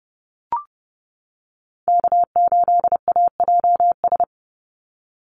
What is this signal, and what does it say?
Morse code, heard as a keyed tone, about 30 wpm, 700 Hz. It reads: K8AJS
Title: Call Signs - 30wpm